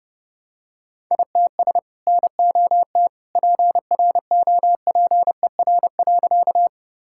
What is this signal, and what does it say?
Morse code, heard as a keyed tone, about 32 wpm, 700 Hz. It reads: ITS NOT PROPER.
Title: Sentences from Top 1000 Words - Review Part 5 - 30wpm